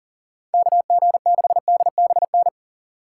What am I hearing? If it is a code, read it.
Morse code, heard as a keyed tone, about 48 wpm, 700 Hz. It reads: KG6DBN
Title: Speed Racing - USA Call Signs 40wpm